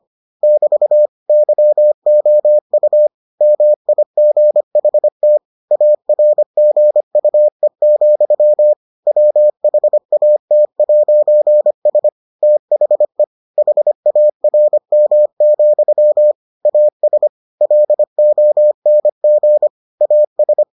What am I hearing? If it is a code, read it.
Morse code, heard as a keyed tone, about 25 wpm, 600 Hz. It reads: <BT> YOU MIGHT ARGUE, WHAT'S THE HARM, AS LONG AS